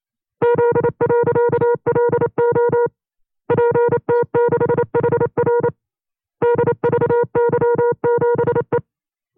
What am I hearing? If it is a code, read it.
Morse code, heard as a keyed tone, about 28 wpm, 480 Hz. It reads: Z.LO PT6HR DVY7E
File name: FreeFiles / CW-practice / 25WPM-side1